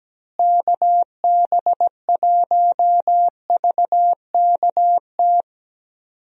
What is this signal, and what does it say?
Morse code, heard as a keyed tone, about 17 wpm, 700 Hz. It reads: KB1VKT
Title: USA Call Signs - 6 Characters - Repeated 3x - 17wpm